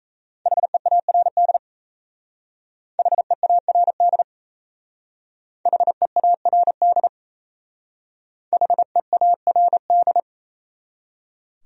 Morse code, heard as a keyed tone, about 35 wpm, 700 Hz. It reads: HEARD HEARD HEARD HEARD
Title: Speed-Racing Top 500 Words 28wpm